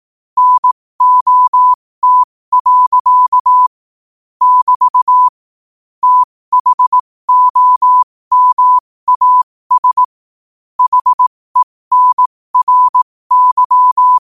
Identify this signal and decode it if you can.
Morse code, heard as a keyed tone, about 18 wpm, spaced as for 15 wpm, 1000 Hz. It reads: NOT. <BT> THOMAS HENRY